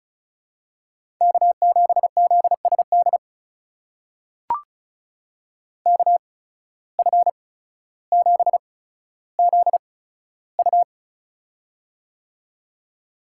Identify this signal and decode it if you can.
Morse code, heard as a keyed tone, about 36 wpm, spaced as for 15 wpm, 700 Hz. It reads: K7ZSD KF7ZU
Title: USA Call Signs - 5 Characters - 8x Character Spacing - 35wpm